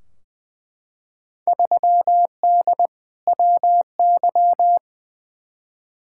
Morse code, heard as a keyed tone, about 20 wpm, 700 Hz. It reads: 3D WY
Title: Field Day Exchanges Encore - 20wpm